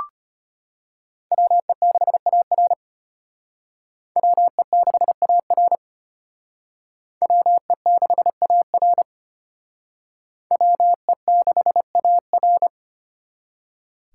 Morse code, heard as a keyed tone, about 31 wpm, 700 Hz. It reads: WE6AR WE6AR WE6AR WE6AR
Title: Speed Racing - USA 2x2 Call Signs 25wpm